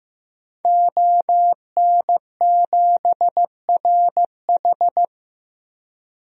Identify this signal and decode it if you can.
Morse code, heard as a keyed tone, about 15 wpm, 700 Hz. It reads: ON7RH
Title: DXCC-100 Call Signs - 5 Characters - 15wpm